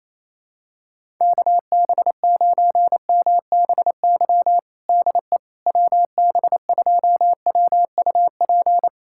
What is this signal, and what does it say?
Morse code, heard as a keyed tone, about 27 wpm, 700 Hz. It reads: KB9MBY DE WB2WUP
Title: USA Callsign DE Callsign - 6 Characters - 28wpm